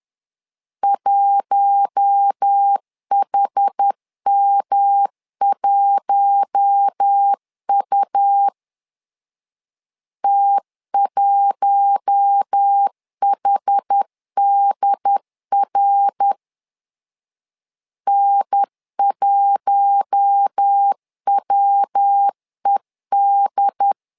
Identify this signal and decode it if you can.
Morse code, heard as a keyed tone, about 10 wpm, 800 Hz. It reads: 1HM1U T1HDR N1WED
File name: LEZIONI / 4 Q1 10wpm 2448